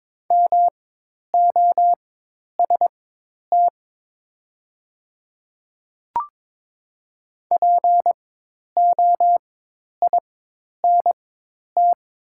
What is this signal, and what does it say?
Morse code, heard as a keyed tone, about 22 wpm, spaced as for 10 wpm, 700 Hz. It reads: MOST POINT
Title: 4x Character Spacing - Top 200 Words - 22wpm